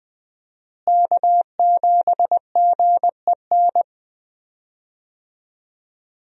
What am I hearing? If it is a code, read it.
Morse code, heard as a keyed tone, about 21 wpm, 700 Hz. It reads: K7GEN
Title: USA Call Signs - 5 Characters - Repeated 3x - 20wpm